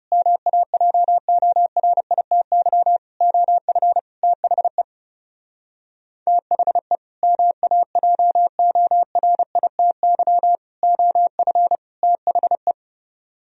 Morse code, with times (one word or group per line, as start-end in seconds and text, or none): 0.1-3.0 MAJORITY
3.2-4.0 OF
4.2-4.8 THE
6.3-7.0 THE
7.2-10.6 MAJORITY
10.8-11.8 OF
12.0-12.7 THE